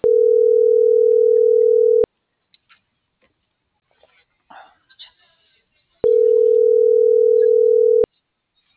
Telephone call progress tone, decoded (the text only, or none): ringback tone